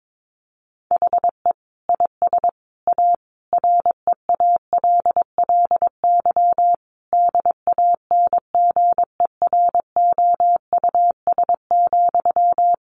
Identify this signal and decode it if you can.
Morse code, heard as a keyed tone, about 22 wpm, 700 Hz. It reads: HE IS A REALLY DANGEROUS,
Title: Sentences from Top 1200 Words 22wpm